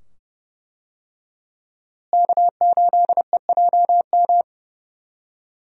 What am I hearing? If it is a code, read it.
Morse code, heard as a keyed tone, about 30 wpm, 700 Hz. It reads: K8EJM
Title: USA Call Signs - 5 Characters - Repeated 2x - 30wpm